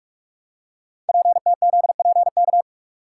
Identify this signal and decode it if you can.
Morse code, heard as a keyed tone, about 44 wpm, 690 Hz. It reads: PTZPK
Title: Code Groups - 5 Letters - 45wpm